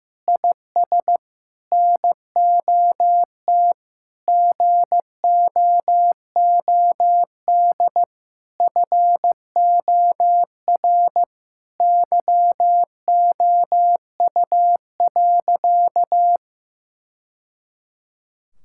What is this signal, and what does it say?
Morse code, heard as a keyed tone, about 16 wpm, 700 Hz. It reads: IS NOT GOOD FOR YOU.
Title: Sentences from Top 1000 Words - Review Part 5 - 15wpm